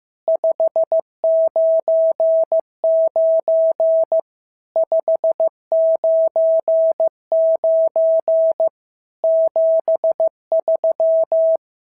Morse code, heard as a keyed tone, about 14 wpm, 650 Hz. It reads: 599 599 73
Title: SOTA QSO Practice 15wpm